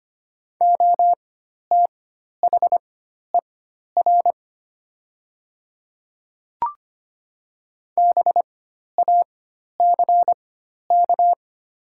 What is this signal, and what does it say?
Morse code, heard as a keyed tone, about 25 wpm, spaced as for 12 wpm, 700 Hz. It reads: OTHER BACK
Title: 4x Character Spacing - Top 200 Words - 25wpm